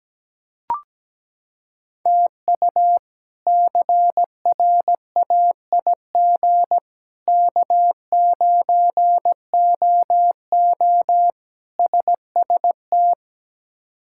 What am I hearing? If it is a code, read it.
Morse code, heard as a keyed tone, about 17 wpm, 700 Hz. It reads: TU CRAIG K9OO SST
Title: K1USN SST QSO Practice - 17wpm